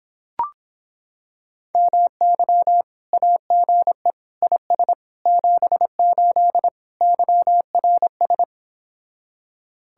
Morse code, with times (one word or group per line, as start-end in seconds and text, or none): 1.7-2.8 MY
3.1-4.1 AGE
4.4-4.9 IS
5.3-6.7 78
7.0-8.4 YRS